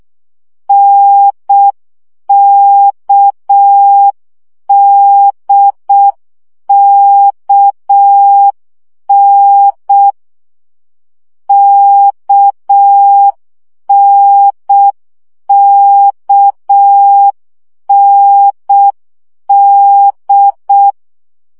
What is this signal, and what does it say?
Morse code, heard as a keyed tone, about 6 wpm, 800 Hz. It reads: NKDKN KNKND
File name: CW gyakorlas n d k 5 baud